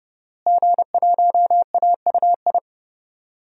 Morse code, heard as a keyed tone, about 32 wpm, 700 Hz. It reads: G1AUI
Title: DXCC-100 Call Signs - 5 Characters - Repeated 2x - 30wpm